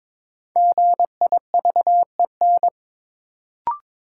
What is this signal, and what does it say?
Morse code, heard as a keyed tone, about 22 wpm, 700 Hz. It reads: GIVEN
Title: Sentences from Top 1400 Words - New Word Drill Alt - 22wpm